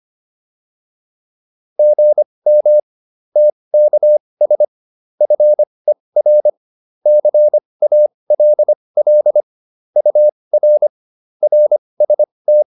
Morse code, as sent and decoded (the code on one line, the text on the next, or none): --. -- / - -.- ... / ..-. .-. / -.-. .- .-.. .-.. / ..- .-. / .-. ... -
GM TKS FER CALL UR RST